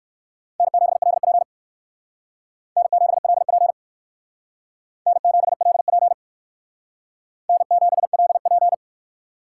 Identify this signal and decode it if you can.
Morse code, heard as a keyed tone, about 56 wpm, 700 Hz. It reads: N7LP N7LP N7LP N7LP